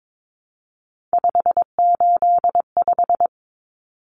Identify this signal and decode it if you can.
Morse code, heard as a keyed tone, about 22 wpm, 700 Hz. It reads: 585
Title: Numbers - 3-Digit 22wpm